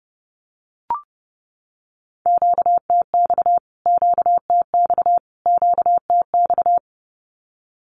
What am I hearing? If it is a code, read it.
Morse code, heard as a keyed tone, about 30 wpm, 700 Hz. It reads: QTX QTX QTX
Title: Q-Codes Repeated 3x 30wpm